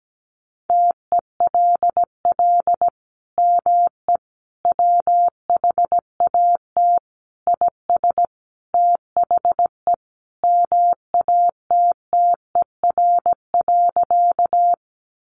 Morse code, heard as a keyed tone, about 17 wpm, 700 Hz. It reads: TELL ME WHAT IS THE MATTER.
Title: Sentences from Top 700 Words 17wpm